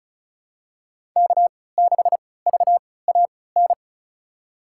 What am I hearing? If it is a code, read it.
Morse code, heard as a keyed tone, about 36 wpm, spaced as for 20 wpm, 690 Hz. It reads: K6VAN
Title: USA Call Signs - 5 Characters - 3x Character Spacing - 35wpm